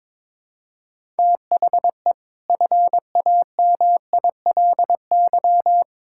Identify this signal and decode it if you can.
Morse code, heard as a keyed tone, about 21 wpm, 700 Hz. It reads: THE FAMILY
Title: Sentences from Top 1000 Words - Review Part 6 - 22wpm